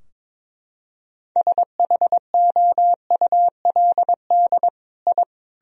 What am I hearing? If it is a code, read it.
Morse code, heard as a keyed tone, about 22 wpm, 700 Hz. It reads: SHOULD I
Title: Sentences from Top 1000 Words - Review Part 4 - 22wpm